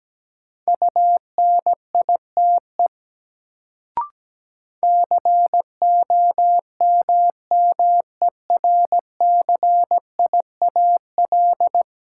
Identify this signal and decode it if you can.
Morse code, heard as a keyed tone, about 17 wpm, 700 Hz. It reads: UNITE COMMERCIAL
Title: Sentences from Top 1000 Words - Review Part 5 - New Word Drill - 17wpm